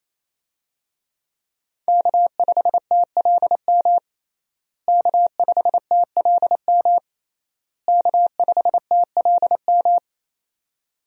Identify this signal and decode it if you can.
Morse code, heard as a keyed tone, about 27 wpm, 700 Hz. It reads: K5TLM K5TLM K5TLM
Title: USA Call Signs - 5 Characters - Repeated 3x - 28wpm